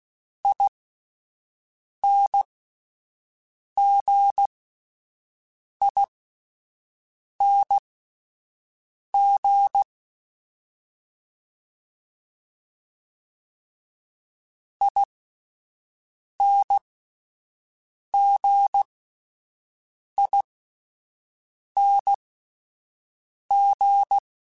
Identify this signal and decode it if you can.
Morse code, heard as a keyed tone, about 16 wpm, spaced as for 5 wpm, 800 Hz. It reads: INGING INGING